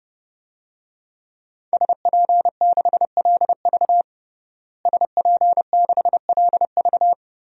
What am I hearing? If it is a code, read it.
Morse code, heard as a keyed tone, about 30 wpm, 700 Hz. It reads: SP6LV SP6LV
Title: DXCC-100 Call Signs - 5 Characters - Repeated 2x - 30wpm